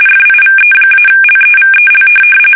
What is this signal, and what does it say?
RTTY, about 45.45 baud. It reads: FIVE-LEVEL
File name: fsk850 / teletype-corp